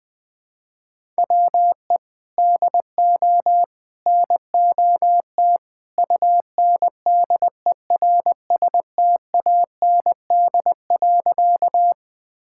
Morse code, heard as a keyed tone, about 19 wpm, 700 Hz. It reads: WE DO NOT UNDERSTAND.
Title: Sentences with 3-Word Phrases - 20wpm